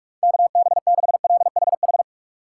Morse code, heard as a keyed tone, about 46 wpm, 700 Hz. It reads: KB6LHH
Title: USA Call Signs - 3x Character Spacing - 45wpm